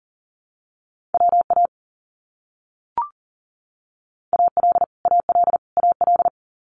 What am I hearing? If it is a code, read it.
Morse code, heard as a keyed tone, about 40 wpm, 700 Hz. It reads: WA AL AL AL
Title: US States Repeated 3x 40wpm